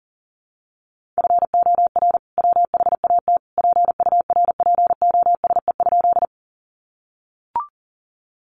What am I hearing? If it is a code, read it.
Morse code, heard as a keyed tone, about 40 wpm, 700 Hz. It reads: FOR WHAT PURPOSE?
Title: Sentences from Top 800 Words 40wpm